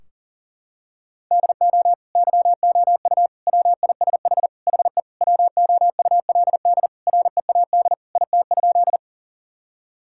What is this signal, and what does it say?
Morse code, heard as a keyed tone, about 40 wpm, 700 Hz. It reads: DO YOU WISH HE WOULD READ IT?
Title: Sentences from Top 1600 Words 40wpm